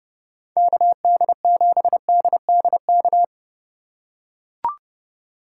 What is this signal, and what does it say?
Morse code, heard as a keyed tone, about 30 wpm, 700 Hz. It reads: KD7DDK